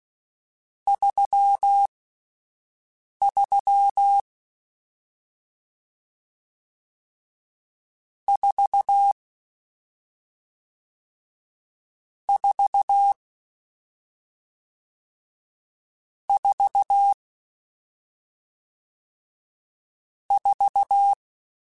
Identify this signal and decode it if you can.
Morse code, heard as a keyed tone, about 16 wpm, spaced as for 5 wpm, 800 Hz. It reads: 33 4 4 4 4